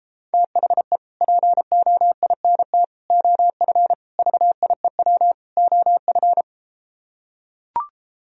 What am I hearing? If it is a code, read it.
Morse code, heard as a keyed tone, about 34 wpm, 700 Hz. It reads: THE POINT OF VIEW OF